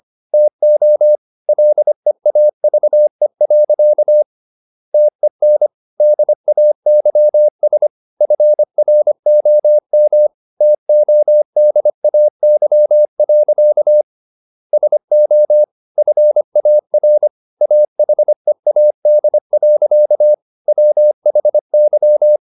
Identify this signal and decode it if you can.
Morse code, heard as a keyed tone, about 25 wpm, 600 Hz. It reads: TO LEAVE. TEN DAYS FROM TODAY. SO FAR AHEAD. WHY